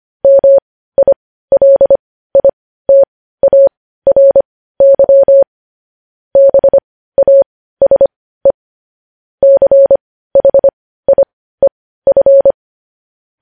Morse code, heard as a keyed tone, about 25 wpm, spaced as for 15 wpm, 550 Hz. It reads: MILITARY BASE CHIEF
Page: news-headlines-in-morse-code-at-15-wpm